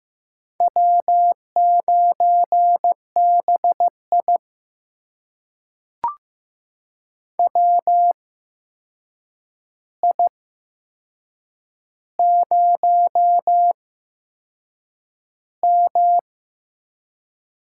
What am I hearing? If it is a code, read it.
Morse code, heard as a keyed tone, about 15 wpm, spaced as for 6 wpm, 700 Hz. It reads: W9BI WI0M
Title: USA Call Signs - 4 Characters - 8x Character Spacing - 15wpm